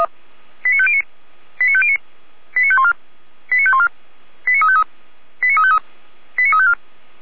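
Five-tone selective call (ZVEI), heard as quick runs of five tones, tones of about 70 ms. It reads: 89509 89509 89514 89514 89353 89353 89354
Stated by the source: ZVEI five-tone call 89164 received on an SDR, fire brigade Feuerwehr Orlamünde.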